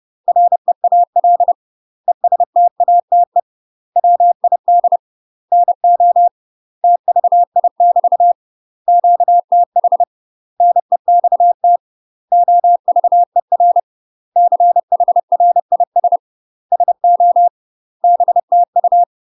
Morse code, heard as a keyed tone, about 30 wpm, 700 Hz. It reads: REAL ESTATE WID NO TVI <BT> QTH NEXT OVER CHRIS SO BTU
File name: UK MASTER QSOs TRAINING / 30WPM / UK MASTER QSOs 0021